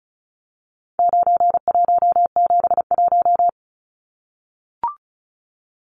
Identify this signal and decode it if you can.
Morse code, heard as a keyed tone, about 34 wpm, 700 Hz. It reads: 9171
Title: Numbers - 4-Digit 35wpm